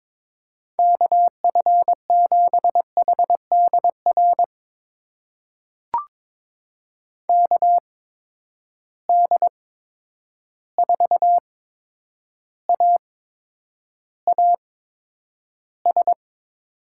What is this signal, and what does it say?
Morse code, heard as a keyed tone, about 22 wpm, spaced as for 9 wpm, 700 Hz. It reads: KF7HDR KD4AAS
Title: USA Call Signs - 6 Characters - 8x Character Spacing - 22wpm